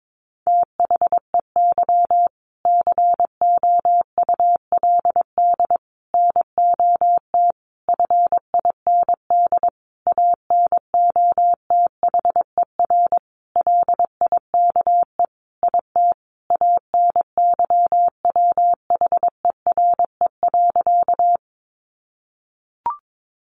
Morse code, heard as a keyed tone, about 22 wpm, 700 Hz. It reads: THEY COULD NOT FIND ANOTHER LIKE IT ANYWHERE.
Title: Sentences from Top 1400 Words 22wpm